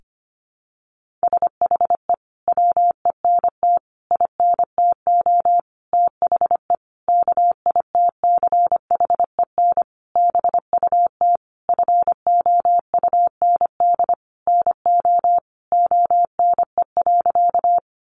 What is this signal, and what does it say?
Morse code, heard as a keyed tone, about 25 wpm, 700 Hz. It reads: SHE WENT INTO THE KITCHEN BUT FOUND NO ONE.
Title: Sentences from Top 1400 Words 25wpm